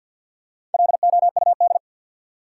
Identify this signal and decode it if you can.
Morse code, heard as a keyed tone, about 53 wpm, 700 Hz. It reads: LOUD
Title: Four Letter Words - 50wpm